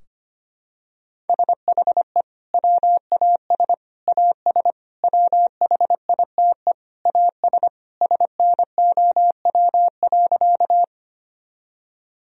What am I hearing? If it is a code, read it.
Morse code, heard as a keyed tone, about 25 wpm, 700 Hz. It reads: SHE WAS AS WHITE AS SNOW.
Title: Sentences from Top 1000 Words - Review Part 3 - 25wpm